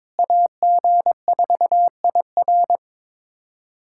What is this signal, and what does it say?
Morse code, heard as a keyed tone, about 22 wpm, 700 Hz. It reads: AG4IR
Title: USA Callsign DE Callsign 22wpm